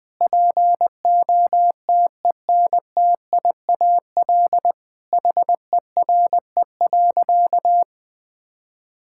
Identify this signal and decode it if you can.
Morse code, heard as a keyed tone, about 20 wpm, 700 Hz. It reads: POTENTIAL HERE.
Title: Sentences with 6-Word Phrases - 20wpm